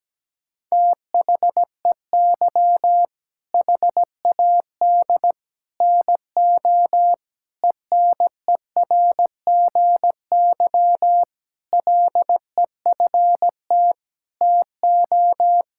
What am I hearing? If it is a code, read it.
Morse code, heard as a keyed tone, about 17 wpm, 700 Hz. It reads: THEY HAD NO ENERGY LEFT TO
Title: Sentences from Top 1000 Words - Review Part 6 - 17wpm